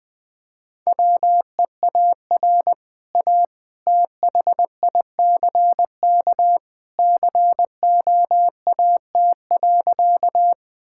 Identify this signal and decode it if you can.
Morse code, heard as a keyed tone, about 20 wpm, 700 Hz. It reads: WEAR A THICK COAT.
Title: Easy Sentences with Hints - 20wpm